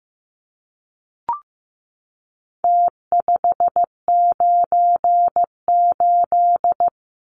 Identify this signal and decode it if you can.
Morse code, heard as a keyed tone, about 15 wpm, 700 Hz. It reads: T598
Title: Numbers - Contest Style - 4-Digit 15wpm